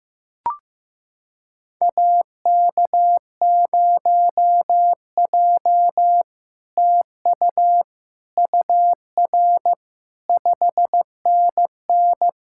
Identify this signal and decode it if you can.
Morse code, heard as a keyed tone, about 16 wpm, 700 Hz. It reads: AK0J TU UR 5NN
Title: SOTA QSO Practice 15wpm